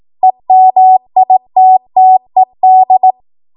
Morse code, heard as a keyed tone, about 18 wpm, 750 Hz. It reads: EMITTED
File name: Morse / 18WPM